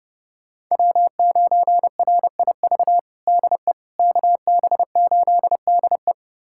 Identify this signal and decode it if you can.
Morse code, heard as a keyed tone, about 32 wpm, 700 Hz. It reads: W9RIV DE KB8DE
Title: USA Callsign DE Callsign - 5 Characters - 30wpm